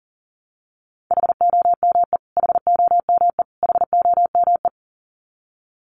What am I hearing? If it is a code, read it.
Morse code, heard as a keyed tone, about 40 wpm, 700 Hz. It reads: HOME HOME HOME
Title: Top 200 Words Repeated 40wpm